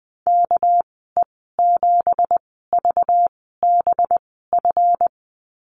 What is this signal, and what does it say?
Morse code, heard as a keyed tone, about 20 wpm, spaced as for 14 wpm, 700 Hz. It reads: KE7VBF
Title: 213 Lesson 34g - Number 8 - Call Signs 20wpm